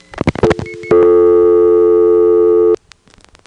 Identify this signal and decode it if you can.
Dial tone (North America), 350 + 440 Hz.